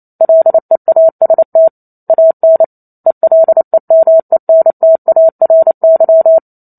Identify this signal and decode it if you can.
Morse code, heard as a keyed tone, about 28 wpm, 650 Hz. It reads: LEAST AN ELEMENTARY
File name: Chapter 2 - 28 WPM